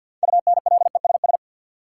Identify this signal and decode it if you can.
Morse code, heard as a keyed tone, about 53 wpm, 700 Hz. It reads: UNLESS
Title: Sentences from Top 1300 Words - New Word Drill Alt - 50wpm